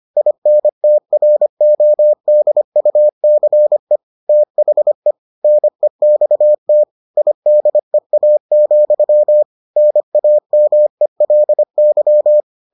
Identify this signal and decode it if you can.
Morse code, heard as a keyed tone, about 25 wpm, 600 Hz. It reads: INTRODUCE THE NEXT IDEA, NAMELY